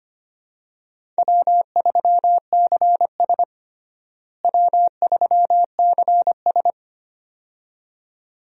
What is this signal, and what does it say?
Morse code, heard as a keyed tone, about 25 wpm, 700 Hz. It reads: W3CS W3CS
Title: USA Call Signs - 4 Characters - Repeated 2x - 25wpm